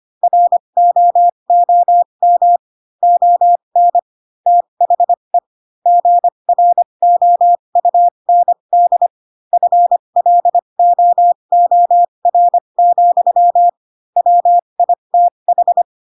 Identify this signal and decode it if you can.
Morse code, heard as a keyed tone, about 25 wpm, spaced as for 22 wpm, 700 Hz. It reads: ROOM ON THE GROUND FLOOR, WITH
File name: Pinocchio / Pinocchio ch3 18 25 ik1hge